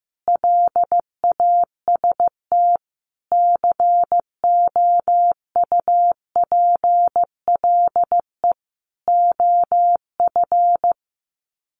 Morse code, heard as a keyed tone, about 15 wpm, 700 Hz. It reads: LAST COUPLE OF
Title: Sets of 5 Words Encore 15wpm